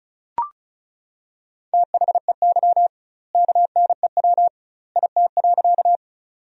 Morse code, heard as a keyed tone, about 36 wpm, 700 Hz. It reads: THEY KNEW IT.
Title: 2x Word-Spacing - Sentences from Top 400 Words - 35wpm